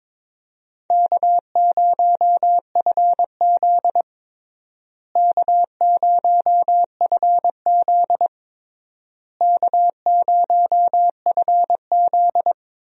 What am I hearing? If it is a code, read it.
Morse code, heard as a keyed tone, about 22 wpm, 700 Hz. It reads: K0FZ K0FZ K0FZ
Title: USA Call Signs - 4 Characters - Repeated 3x - 22wpm